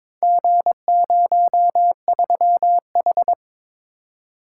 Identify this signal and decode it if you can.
Morse code, heard as a keyed tone, about 22 wpm, 700 Hz. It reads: G03H
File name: practice / 22wpm / Four Letter-Number Combinations - 22wpm